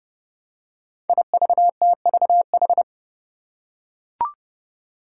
Morse code, heard as a keyed tone, about 32 wpm, 700 Hz. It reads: IVTVH
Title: Five Challenging Letter-Number Combinations 30wpm